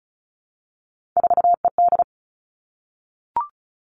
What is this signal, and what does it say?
Morse code, heard as a keyed tone, about 36 wpm, 700 Hz. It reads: 4ED
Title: Three Challenging Letter-Number Combinations 35wpm